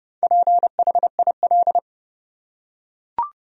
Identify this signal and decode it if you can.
Morse code, heard as a keyed tone, about 30 wpm, 700 Hz. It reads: PHIL
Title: USA Common Names - 30wpm